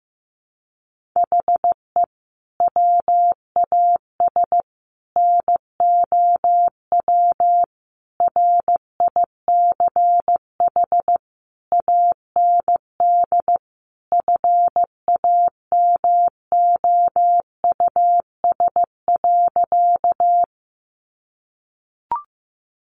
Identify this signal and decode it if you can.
Morse code, heard as a keyed tone, about 15 wpm, 700 Hz. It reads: HE WAS NOW RICH AND FAMOUS.
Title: Sentences from Top 1500 Words 15wpm